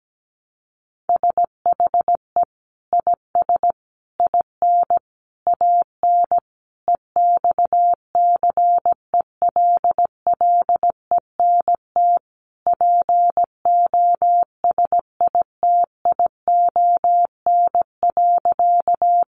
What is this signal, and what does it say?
Morse code, heard as a keyed tone, about 17 wpm, 700 Hz. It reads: SHE IS IN AN EXCELLENT POSITION.
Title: Sentences from Top 1000 Words 17wpm